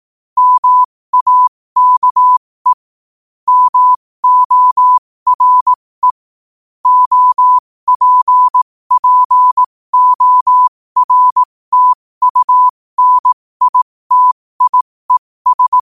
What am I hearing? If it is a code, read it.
Morse code, heard as a keyed tone, about 18 wpm, spaced as for 15 wpm, 1000 Hz. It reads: MAKE MORE OPPORTUNITIES
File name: qotdfiles / 15wpm